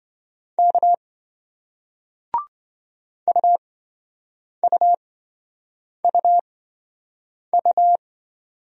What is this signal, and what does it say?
Morse code, heard as a keyed tone, about 27 wpm, spaced as for 7 wpm, 700 Hz. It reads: K UUUU